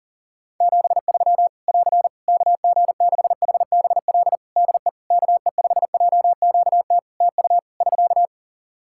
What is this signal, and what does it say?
Morse code, heard as a keyed tone, about 40 wpm, 700 Hz. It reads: 73 <AR> KG6HBL DE KE5JQT TU <SK>
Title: Practice QSO - Exchange 4 - Ending - 40wpm